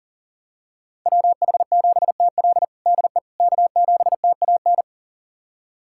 Morse code, heard as a keyed tone, about 40 wpm, 700 Hz. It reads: WH7TL DE K7TAN
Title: USA Callsign DE Callsign - 5 Characters - 40wpm